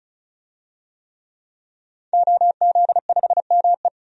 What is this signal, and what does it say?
Morse code, heard as a keyed tone, about 35 wpm, 700 Hz. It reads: OZ5ME
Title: DXCC-100 Call Signs - 5 Characters - 35wpm